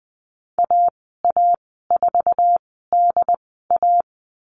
Morse code, heard as a keyed tone, about 20 wpm, spaced as for 14 wpm, 700 Hz. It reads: AA4DA
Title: 082 Lesson 14g - Number 2 - Call Signs 20wpm